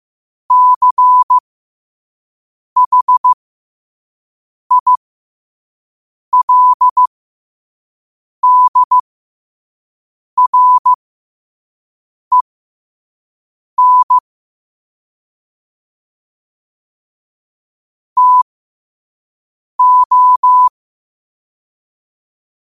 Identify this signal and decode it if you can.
Morse code, heard as a keyed tone, about 15 wpm, spaced as for 5 wpm, 1000 Hz. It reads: CHILDREN TO
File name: qotdfiles / 5wpm